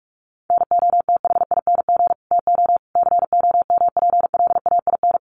Morse code, heard as a keyed tone, about 44 wpm, 700 Hz. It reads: NOTHING TO COMPLAIN
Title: Sentences from Top 1300 Words 45wpm